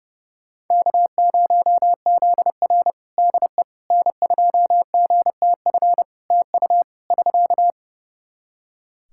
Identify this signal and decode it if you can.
Morse code, heard as a keyed tone, about 28 wpm, 700 Hz. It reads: K0ZR DE N2GTF TU <SK>
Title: Practice QSO - Exchange 4 - Ending - 30wpm